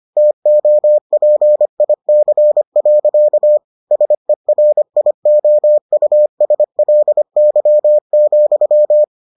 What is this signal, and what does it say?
Morse code, heard as a keyed tone, about 25 wpm, 600 Hz. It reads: TOPIC. SERIOUSLY,